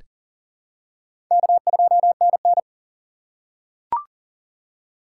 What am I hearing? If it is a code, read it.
Morse code, heard as a keyed tone, about 40 wpm, 700 Hz. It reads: K2NN
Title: USA Call Signs - 3x Character Spacing - 40wpm